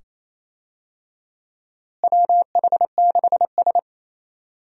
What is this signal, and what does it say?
Morse code, heard as a keyed tone, about 28 wpm, 700 Hz. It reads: WH6S
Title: USA Call Signs - 4 Characters - Repeated 2x - 28wpm